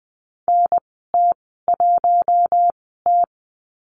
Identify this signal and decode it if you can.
Morse code, heard as a keyed tone, about 20 wpm, spaced as for 14 wpm, 700 Hz. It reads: NT1T
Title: 042 Lesson 8g - Number 1 - Call Signs 20wpm